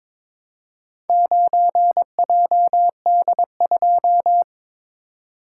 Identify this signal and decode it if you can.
Morse code, heard as a keyed tone, about 22 wpm, 700 Hz. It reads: 9JD2
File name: practice / 22wpm / Four Letter-Number Combinations - 22wpm